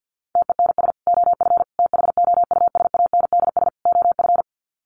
Morse code, heard as a keyed tone, about 50 wpm, 700 Hz. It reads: TENS OF THOUSANDS OF
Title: Sets of 4 Words 50wpm